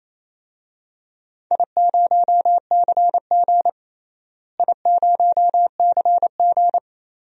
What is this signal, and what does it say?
Morse code, heard as a keyed tone, about 28 wpm, 700 Hz. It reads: I0CG I0CG
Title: DXCC-100 Call Signs - 4 Characters - Repeated 2x - 28wpm